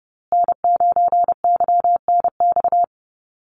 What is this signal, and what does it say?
Morse code, heard as a keyed tone, about 30 wpm, 700 Hz. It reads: N9YNX
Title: USA Call Signs Repeated 2x 30wpm